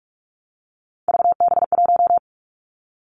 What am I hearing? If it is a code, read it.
Morse code, heard as a keyed tone, about 46 wpm, 700 Hz. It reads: VB1
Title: Three Challenging Letter-Number Combinations 45wpm